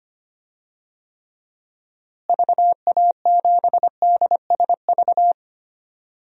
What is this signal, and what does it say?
Morse code, heard as a keyed tone, about 25 wpm, 700 Hz. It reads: VA7DSV
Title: DXCC-100 Call Signs - 6 Characters - Repeated 3x - 25wpm